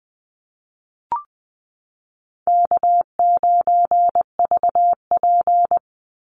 Morse code, heard as a keyed tone, about 20 wpm, 700 Hz. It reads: K9VP
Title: Easy Call Signs Repeated 2x - 20wpm